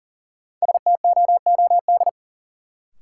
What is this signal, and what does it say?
Morse code, heard as a keyed tone, about 40 wpm, 700 Hz. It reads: STOOD
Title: Sentences from Top 500 Words - New Word Drill - 40wpm